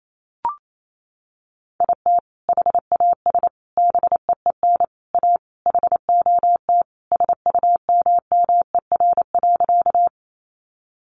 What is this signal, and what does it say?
Morse code, heard as a keyed tone, about 28 wpm, 700 Hz. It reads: IT HAS BEEN A HOT SUMMER.
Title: Sentences from Top 1100 Words 28wpm